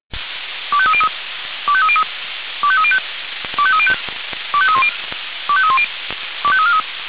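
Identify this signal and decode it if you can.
Five-tone selective call (ZVEI), heard as quick runs of five tones, tones of about 70 ms. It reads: 25303 25303 25305 25305 25310 25310 25343
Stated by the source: ZVEI five-tone selective call received on SDR.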